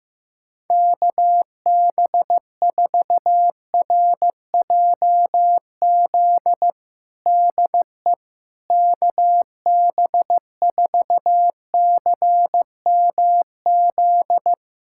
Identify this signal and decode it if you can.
Morse code, heard as a keyed tone, about 15 wpm, 700 Hz. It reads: KB4RJZ DE KB4CMZ
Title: USA Callsign DE Callsign - 6 Characters - 15wpm